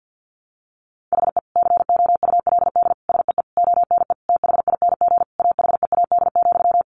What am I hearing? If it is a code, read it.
Morse code, heard as a keyed tone, about 50 wpm, 700 Hz. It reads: HE COULD SEE ONE THING AHEAD,
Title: Sentences from Top 1100 Words 50wpm